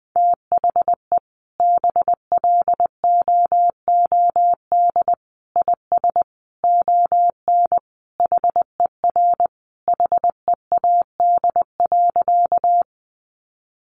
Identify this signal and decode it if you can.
Morse code, heard as a keyed tone, about 20 wpm, 700 Hz. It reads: THE BLOOD IS ON HER HEAD.
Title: Sentences from Top 1100 Words 20wpm